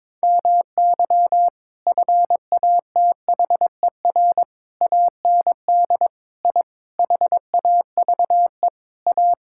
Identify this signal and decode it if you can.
Morse code, heard as a keyed tone, about 22 wpm, 700 Hz. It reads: MY FATHER AND I HAVE A
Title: Sentences from Top 1000 Words - Review Part 6 - 22wpm